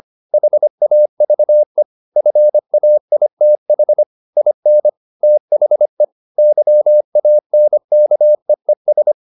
Morse code, heard as a keyed tone, about 25 wpm, 600 Hz. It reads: HAVE FAITH IN THE YANKEES
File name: the old man and the sea part 1 25wpm